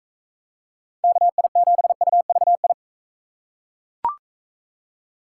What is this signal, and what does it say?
Morse code, heard as a keyed tone, about 41 wpm, 700 Hz. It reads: KI7UVI